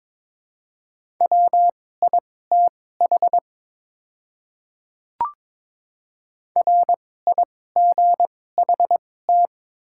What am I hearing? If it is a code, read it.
Morse code, heard as a keyed tone, about 23 wpm, spaced as for 16 wpm, 700 Hz. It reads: WITH RIGHT
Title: 2x Character Spacing - Top 300 Words - 22wpm